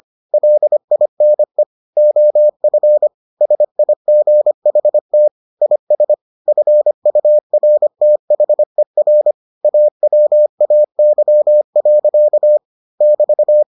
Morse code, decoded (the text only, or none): LINE OF SIGHT IS FURTHER AWAY. <BT>